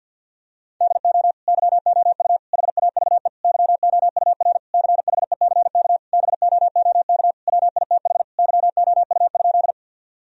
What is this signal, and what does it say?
Morse code, heard as a keyed tone, about 50 wpm, 700 Hz. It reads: DO YOU HAVE YOUR CHECK BOOK WITH YOU?
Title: Sentences from Top 1000 Words - Review Part 3 - 50wpm